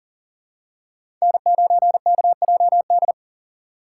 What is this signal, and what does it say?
Morse code, heard as a keyed tone, about 40 wpm, 690 Hz. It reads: N9KJD
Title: USA Call Signs - 5 Characters - 6x Character Spacing - 40wpm